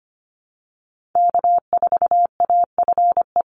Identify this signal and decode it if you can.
Morse code, heard as a keyed tone, about 25 wpm, 700 Hz. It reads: K4AFE
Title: Call Signs Repeated 2x - 25wpm